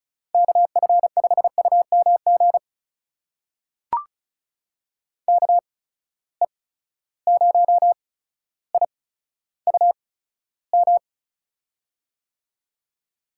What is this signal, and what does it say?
Morse code, heard as a keyed tone, about 36 wpm, spaced as for 15 wpm, 700 Hz. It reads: KF5UMG KE0IUM
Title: USA Call Signs - 6 Characters - 8x Character Spacing - 35wpm